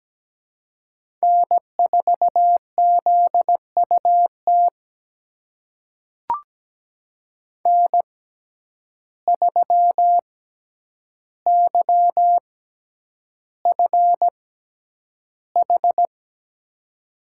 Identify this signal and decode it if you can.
Morse code, heard as a keyed tone, about 17 wpm, spaced as for 9 wpm, 700 Hz. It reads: N4ZUT N3YFH
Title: USA Call Signs - 5 Characters - 6x Character Spacing - 17wpm